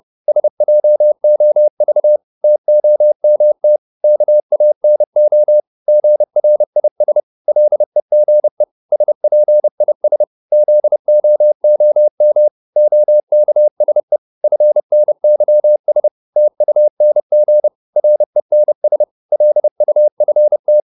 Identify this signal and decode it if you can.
Morse code, heard as a keyed tone, about 30 wpm, 600 Hz. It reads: SJOV TOMT KANO GRIS LEGE SPIS ZOOM OKSE FNYS TUNG RENS LUFT
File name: Korte ord DK 30wpm